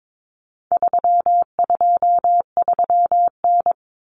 Morse code, heard as a keyed tone, about 22 wpm, 700 Hz. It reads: 323N
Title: Common Cut Numbers - 4-Digit 22wpm